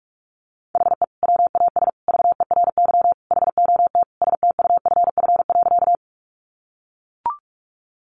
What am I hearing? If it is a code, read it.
Morse code, heard as a keyed tone, about 44 wpm, 700 Hz. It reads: HE WAS VERY HOT STUFF.
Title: Sentences from Top 1400 Words 45wpm